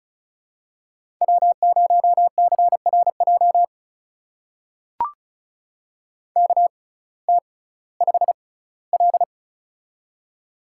Morse code, heard as a keyed tone, about 36 wpm, spaced as for 19 wpm, 700 Hz. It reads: W0CRJ KT5L
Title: USA Call Signs - 6x Character Spacing - 35wpm